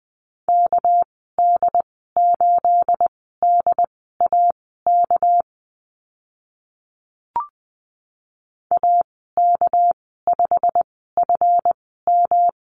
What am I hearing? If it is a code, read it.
Morse code, heard as a keyed tone, about 20 wpm, spaced as for 15 wpm, 700 Hz. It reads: KD8DAK AK5FM
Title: 231 Lesson 37g - Letter x - Call Signs 20wpm